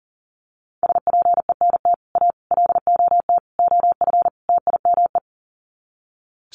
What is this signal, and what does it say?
Morse code, heard as a keyed tone, about 40 wpm, 700 Hz. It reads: SPENT A LOT OF TIME
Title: Sets of 5 Words 40wpm